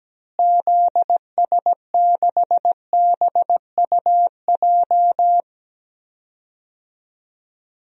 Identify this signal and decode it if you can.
Morse code, heard as a keyed tone, about 17 wpm, 700 Hz. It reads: ZS6BUJ
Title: DXCC-100 Call Signs - 6 Characters - Repeated 3x - 17wpm